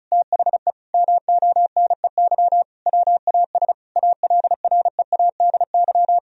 Morse code, heard as a keyed tone, about 36 wpm, 700 Hz. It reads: THE MONEY WAS ALREADY
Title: Sentences from Top 1000 Words - Review Part 1 - 35wpm